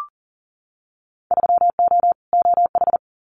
Morse code, heard as a keyed tone, about 40 wpm, 700 Hz. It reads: 3O OH
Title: Winter Field Day Exchanges 40wpm